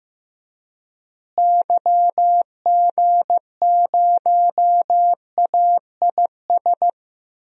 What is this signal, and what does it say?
Morse code, heard as a keyed tone, about 15 wpm, 700 Hz. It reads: YG0AIS
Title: DXCC-100 Call Signs - 6 Characters - Repeated 3x - 15wpm